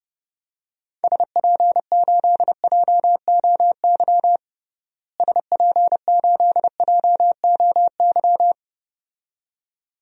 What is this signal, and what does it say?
Morse code, heard as a keyed tone, about 32 wpm, 700 Hz. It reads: SP8JOY SP8JOY
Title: DXCC-100 Call Signs - 6 Characters - Repeated 2x - 30wpm